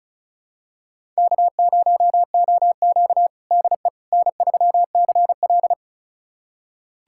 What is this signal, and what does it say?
Morse code, heard as a keyed tone, about 36 wpm, 700 Hz. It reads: K0OQ DE N3CL
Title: USA Callsign DE Callsign - 4 Characters - 35wpm